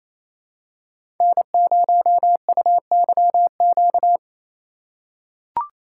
Morse code, heard as a keyed tone, about 28 wpm, 700 Hz. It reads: N0UYQ
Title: USA Call Signs - 5 Characters - 2x Character Spacing - 28wpm